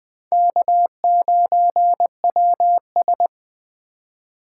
Morse code, heard as a keyed tone, about 20 wpm, 700 Hz. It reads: K9WS
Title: USA Call Signs - 4 Characters - 6x Character Spacing - 20wpm